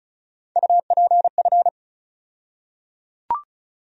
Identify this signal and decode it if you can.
Morse code, heard as a keyed tone, about 34 wpm, 700 Hz. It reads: UPF
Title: USA Call Sign Postfixes - 3 Letters 35wpm